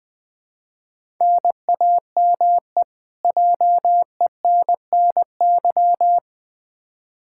Morse code, heard as a keyed tone, about 20 wpm, 700 Hz. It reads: NAME JENNY
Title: K1USN SST QSO Practice 20wpm